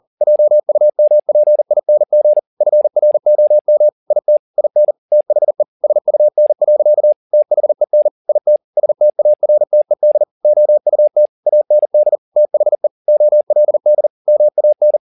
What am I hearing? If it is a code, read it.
Morse code, heard as a keyed tone, about 40 wpm, 600 Hz. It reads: JUMPING FROM IT IN THE SUN. THEN IT STARTED OUT AND THE OLD MAN
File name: the old man and the sea 2 40wpm